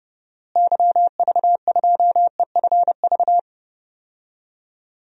Morse code, heard as a keyed tone, about 32 wpm, 700 Hz. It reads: YV2EFV
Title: DXCC-100 Call Signs - 6 Characters - Repeated 2x - 30wpm